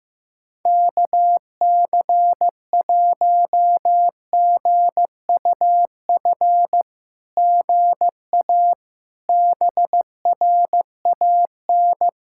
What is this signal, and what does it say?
Morse code, heard as a keyed tone, about 15 wpm, 700 Hz. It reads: KC1GUF GA BRAN
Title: SOTA QSO Practice 15wpm